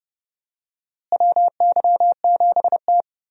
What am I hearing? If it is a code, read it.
Morse code, heard as a keyed tone, about 32 wpm, 700 Hz. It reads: WY7T
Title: USA Call Signs - 4 Characters - Repeated 2x - 30wpm